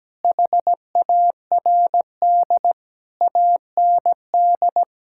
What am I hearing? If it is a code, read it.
Morse code, heard as a keyed tone, about 17 wpm, 700 Hz. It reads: HARD AND
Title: Sentences from Top 1600 Words 17wpm